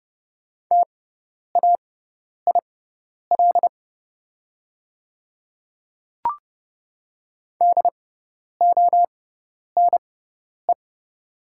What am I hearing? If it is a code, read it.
Morse code, heard as a keyed tone, about 30 wpm, spaced as for 10 wpm, 700 Hz. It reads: TAIL DONE
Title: 6x Character Spacing - Top 500 Words - 30wpm